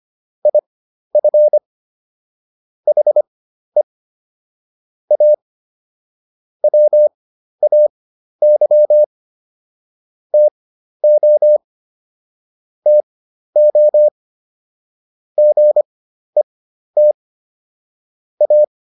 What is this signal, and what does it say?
Morse code, heard as a keyed tone, about 25 wpm, spaced as for 12 wpm, 600 Hz. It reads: IF HE A WAY TO TO GET A